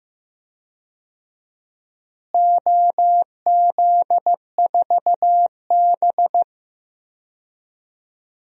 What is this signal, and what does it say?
Morse code, heard as a keyed tone, about 15 wpm, 700 Hz. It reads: OZ4B
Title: DXCC-100 Call Signs - 4 Characters - Repeated 2x - 15wpm